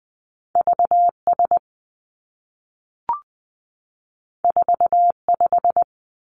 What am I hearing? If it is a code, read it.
Morse code, heard as a keyed tone, about 20 wpm, 700 Hz. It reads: V S 4 5
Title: Two Challenging Letter-Number Combinations 20wpm